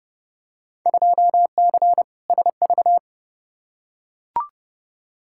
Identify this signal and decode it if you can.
Morse code, heard as a keyed tone, about 30 wpm, 700 Hz. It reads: 2C SV
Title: Field Day Exchanges Encore - 30wpm